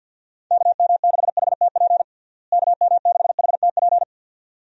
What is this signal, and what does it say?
Morse code, heard as a keyed tone, about 52 wpm, 700 Hz. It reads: KM6HTP KM6HTP
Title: USA Call Signs - 6 Characters - Repeated 2x - 50wpm